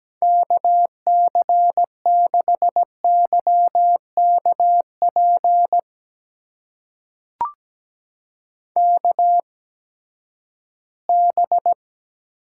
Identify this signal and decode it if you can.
Morse code, heard as a keyed tone, about 17 wpm, spaced as for 12 wpm, 700 Hz. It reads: KC6YKP KB
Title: USA Call Signs - 6 Characters - 8x Character Spacing - 17wpm